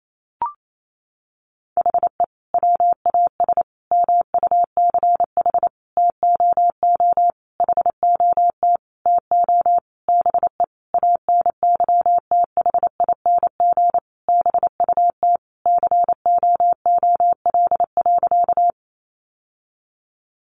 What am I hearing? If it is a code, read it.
Morse code, heard as a keyed tone, about 28 wpm, 700 Hz. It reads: HE WAS MUCH TOO HOT TO BE ANYTHING BUT COOL.
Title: Sentences from Top 1200 Words 28wpm